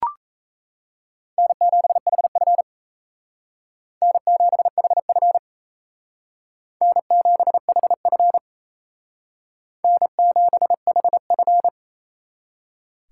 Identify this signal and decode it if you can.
Morse code, heard as a keyed tone, about 35 wpm, 700 Hz. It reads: N7HF N7HF N7HF N7HF
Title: Speed Racing - USA 1x2 Call Signs 28wpm